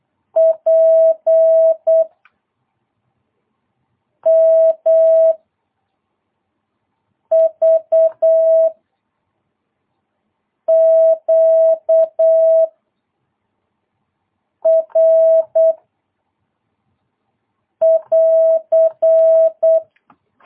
Morse code, heard as a keyed tone, about 8 wpm, 660 Hz. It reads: P M V Q R <AR>
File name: jh8cbh / 2022cw